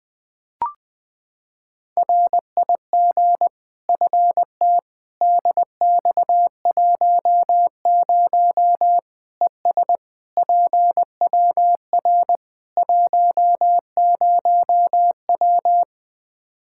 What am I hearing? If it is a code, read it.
Morse code, heard as a keyed tone, about 21 wpm, 700 Hz. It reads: RIG FT DX10 ES PWR 10W
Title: Practice QSO - Exchange 2 - Info - 20wpm